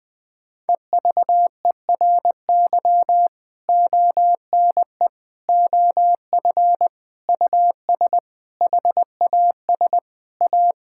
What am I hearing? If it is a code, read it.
Morse code, heard as a keyed tone, about 20 wpm, 700 Hz. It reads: EVERY ONE OF US HAS A
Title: Sentences from Top 1000 Words - Review Part 6 - 20wpm